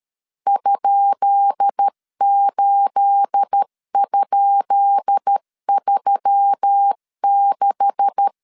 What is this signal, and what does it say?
Morse code, heard as a keyed tone, about 13 wpm, 800 Hz. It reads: ?8?36
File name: LEZIONI / 10 SIGNS / 10B INT 12wpm 2414